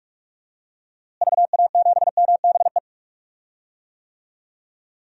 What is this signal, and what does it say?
Morse code, heard as a keyed tone, about 42 wpm, 700 Hz. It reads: VA7MBE
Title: DXCC-100 Call Signs - 6 Characters - Repeated 2x - 45wpm